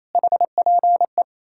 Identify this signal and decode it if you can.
Morse code, heard as a keyed tone, about 38 wpm, 700 Hz. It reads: HPE